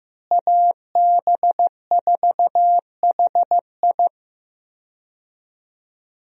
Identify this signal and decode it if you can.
Morse code, heard as a keyed tone, about 15 wpm, 700 Hz. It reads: AB4HI
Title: USA Call Signs - 5 Characters - Repeated 3x - 15wpm